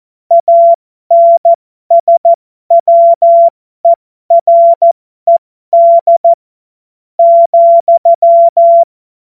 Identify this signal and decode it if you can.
Morse code, heard as a keyed tone, about 14 wpm, spaced as for 11 wpm, 680 Hz. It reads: ANSWERED,
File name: Morsekurs / Witze